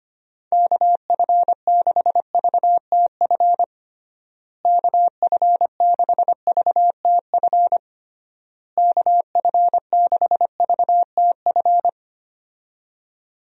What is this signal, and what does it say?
Morse code, heard as a keyed tone, about 25 wpm, 700 Hz. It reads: KF6VTF KF6VTF KF6VTF
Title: USA Call Signs - 6 Characters - Repeated 3x - 25wpm